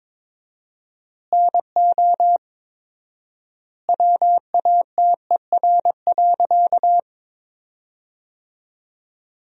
Morse code, heard as a keyed tone, about 22 wpm, 700 Hz. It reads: NO WATER.
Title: 4x Word-Spacing - Sentences from Top 100 Words - 22wpm